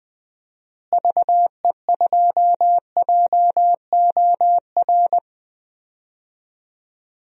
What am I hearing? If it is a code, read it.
Morse code, heard as a keyed tone, about 20 wpm, 700 Hz. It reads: VE2JOR
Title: DXCC-100 Call Signs - 6 Characters - Repeated 2x - 20wpm